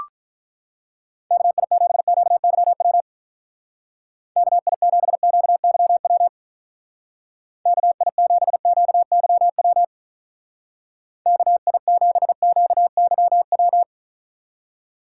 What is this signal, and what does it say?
Morse code, heard as a keyed tone, about 44 wpm, 700 Hz. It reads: KI7QYW KI7QYW KI7QYW KI7QYW